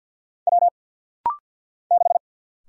Morse code, heard as a keyed tone, about 50 wpm, 700 Hz. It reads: W 6
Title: Single Letter-Number - Rapid-Fire 50wpm